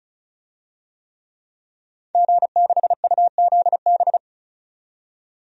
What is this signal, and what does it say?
Morse code, heard as a keyed tone, about 35 wpm, 700 Hz. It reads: G6UZB
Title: DXCC-100 Call Signs - 5 Characters - 35wpm